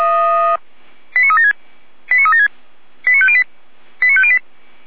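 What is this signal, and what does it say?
Five-tone selective call (ZVEI), heard as quick runs of five tones, tones of about 70 ms. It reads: 89376 89376 89508 89508
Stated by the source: ZVEI five-tone selective call received by SDR, from feuerwehr-orlamuende.org (fire brigade Orlamünde).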